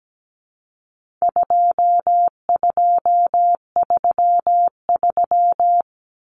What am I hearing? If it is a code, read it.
Morse code, heard as a keyed tone, about 17 wpm, 700 Hz. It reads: 2233
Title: Common Cut Numbers - 4-Digit 17wpm